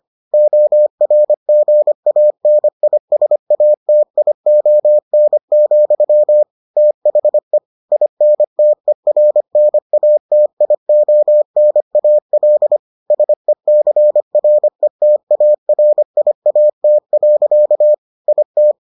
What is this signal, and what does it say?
Morse code, heard as a keyed tone, about 25 wpm, 600 Hz. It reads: ORGANISATION, THE INTERNATIONAL SECRETARIAT. IT